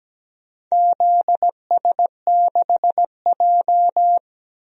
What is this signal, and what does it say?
Morse code, heard as a keyed tone, about 17 wpm, 700 Hz. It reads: ZS6J
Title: DXCC-100 Call Signs - 4 Characters - 17wpm